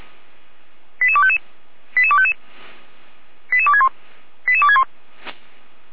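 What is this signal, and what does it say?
Five-tone selective call (ZVEI), heard as quick runs of five tones, tones of about 70 ms. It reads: 88255 88255 88261 88261
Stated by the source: ZVEI five-tone selective call received by SDR.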